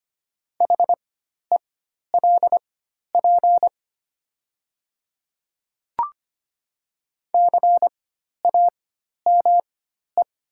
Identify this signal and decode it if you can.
Morse code, heard as a keyed tone, about 25 wpm, spaced as for 12 wpm, 700 Hz. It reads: HELP CAME
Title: 4x Character Spacing - Top 200 Words - 25wpm